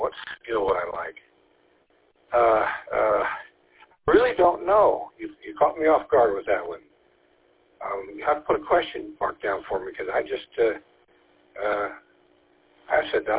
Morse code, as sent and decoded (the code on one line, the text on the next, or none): none